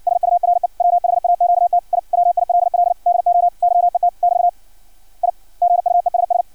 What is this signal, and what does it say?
Morse code, heard as a keyed tone, about 56 wpm, 700 Hz. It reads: SURE WHAQT I WILL DO YET <BT> I GUESS
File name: archived / CRSnet